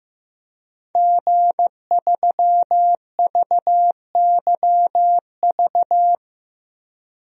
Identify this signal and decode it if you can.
Morse code, heard as a keyed tone, about 15 wpm, 700 Hz. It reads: G3VYV
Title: DXCC-100 Call Signs - 5 Characters - Repeated 3x - 15wpm